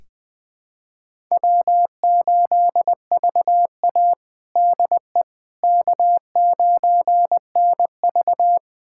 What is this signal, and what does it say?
Morse code, heard as a keyed tone, about 21 wpm, 700 Hz. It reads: W8VA DE K9NV
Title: USA Callsign DE Callsign - 4 Characters - 20wpm